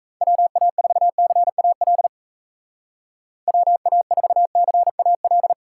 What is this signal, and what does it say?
Morse code, heard as a keyed tone, about 35 wpm, 700 Hz. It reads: WA4CAL WA4CAL
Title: Speed Racing - USA 2x3 Call Signs 28wpm